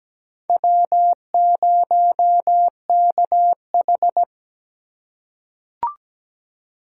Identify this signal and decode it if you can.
Morse code, heard as a keyed tone, about 17 wpm, 700 Hz. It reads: W0KH
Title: USA Call Signs - 4 Characters - 2x Character Spacing - 17wpm